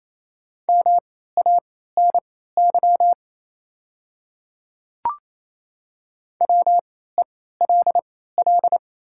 Morse code, heard as a keyed tone, about 27 wpm, spaced as for 16 wpm, 700 Hz. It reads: MANY WELL
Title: 3x Character Spacing - Top 300 Words - 28wpm